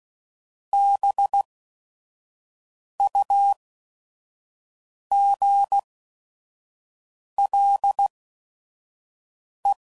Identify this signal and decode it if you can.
Morse code, heard as a keyed tone, about 16 wpm, spaced as for 5 wpm, 800 Hz. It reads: BUGLE